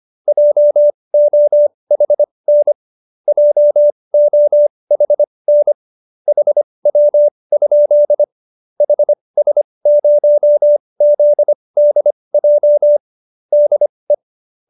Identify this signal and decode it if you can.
Morse code, heard as a keyed tone, about 25 wpm, spaced as for 20 wpm, 590 Hz. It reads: JOHN JOHN HW? HS0ZDJ DE
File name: qso216b 20